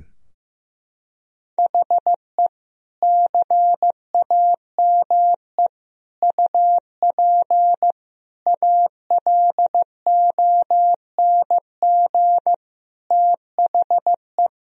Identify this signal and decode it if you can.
Morse code, heard as a keyed tone, about 16 wpm, 700 Hz. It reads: HE CAME UP ALONG THE
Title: Sentences from Top 1000 Words - Review Part 4 - 15wpm